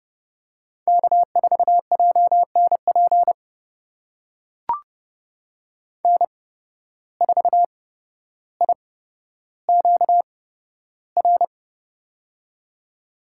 Morse code, heard as a keyed tone, about 30 wpm, spaced as for 13 wpm, 700 Hz. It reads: K4JNP N4IQR
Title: USA Call Signs - 8x Character Spacing - 30wpm